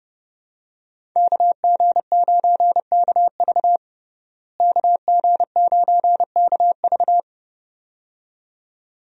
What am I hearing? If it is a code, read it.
Morse code, heard as a keyed tone, about 32 wpm, 700 Hz. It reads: KG9KV KG9KV
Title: USA Call Signs - 5 Characters - Repeated 2x - 30wpm